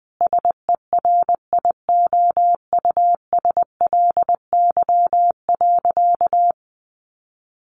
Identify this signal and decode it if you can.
Morse code, heard as a keyed tone, about 20 wpm, 700 Hz. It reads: SERIOUSLY.
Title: Sentences from Top 1200 Words 20wpm